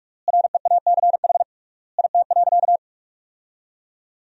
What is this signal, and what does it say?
Morse code, heard as a keyed tone, about 46 wpm, 700 Hz. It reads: REACH IT.
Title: 3x Word-Spacing - Sentences from Top 400 Words - 45wpm